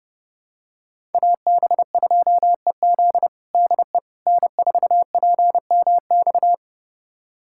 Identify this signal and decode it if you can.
Morse code, heard as a keyed tone, about 30 wpm, 700 Hz. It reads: AB2EZ DE N4PMX
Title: USA Callsign DE Callsign - 5 Characters - 30wpm